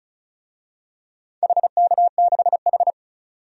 Morse code, heard as a keyed tone, about 36 wpm, 700 Hz. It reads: HK6H
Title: DXCC-100 Call Signs - 4 Characters - 35wpm